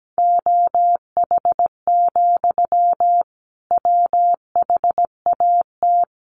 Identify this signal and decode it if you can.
Morse code, heard as a keyed tone, about 17 wpm, 700 Hz. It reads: OH, WHAT
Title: Sentences from Top 1200 Words 17wpm